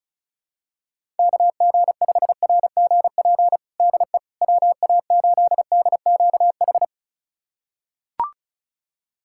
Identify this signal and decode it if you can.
Morse code, heard as a keyed tone, about 34 wpm, 700 Hz. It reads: KG5RGP DE WA8DQH
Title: USA Callsign DE Callsign 35wpm